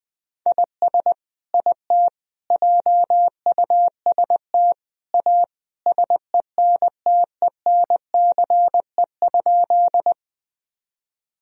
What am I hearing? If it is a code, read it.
Morse code, heard as a keyed tone, about 20 wpm, 700 Hz. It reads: IS IT JUST A SENTENCE?
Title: Sentences from Top 1000 Words - Review Part 1 - 20wpm